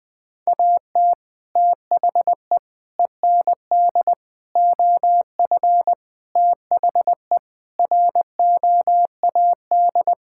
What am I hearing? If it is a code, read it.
Morse code, heard as a keyed tone, about 20 wpm, 700 Hz. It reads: AT THE END OF THE ROAD
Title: Sentences with 3-Word Phrases - Spoken Beginning - 20wpm